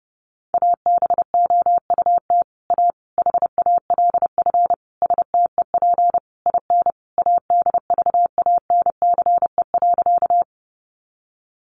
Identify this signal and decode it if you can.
Morse code, heard as a keyed tone, about 30 wpm, 700 Hz. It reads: ABOUT A HALF STEP IN ADVANCE.
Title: Sentences from Top 1400 Words 30wpm